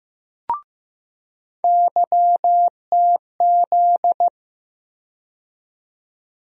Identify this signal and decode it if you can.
Morse code, heard as a keyed tone, about 15 wpm, 700 Hz. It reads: YTZ
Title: USA Call Sign Postfixes - 3 Letters 15wpm